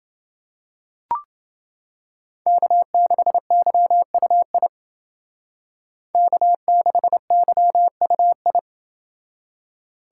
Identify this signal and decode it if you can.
Morse code, heard as a keyed tone, about 29 wpm, 700 Hz. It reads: K6YUI K6YUI